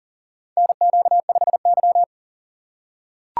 Morse code, heard as a keyed tone, about 40 wpm, 700 Hz. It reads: NQ5Y
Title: USA Call Signs - 4 Characters - 3x Character Spacing - 40wpm